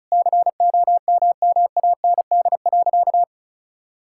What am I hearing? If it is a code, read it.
Morse code, heard as a keyed tone, about 36 wpm, 700 Hz. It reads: COMMAND.
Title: Sentences from Top 1600 Words 35wpm